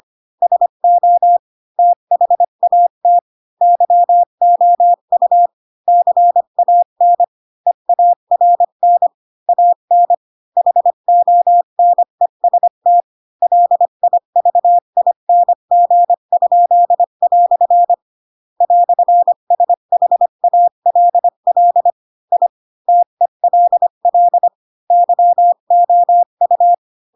Morse code, heard as a keyed tone, about 25 wpm, 700 Hz. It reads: SO THAT YOU CAN EARN AN HONEST LIVING?" "SHALL I TELL YOU
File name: Pinocchio / Pinocchio ch4 20 25 ik1hge